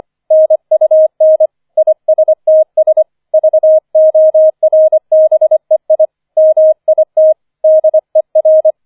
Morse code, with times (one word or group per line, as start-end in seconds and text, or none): 0.3-1.5 NUN
1.8-3.0 ISTS
3.3-6.1 VORBEI
6.4-7.3 MIT
7.6-8.7 DER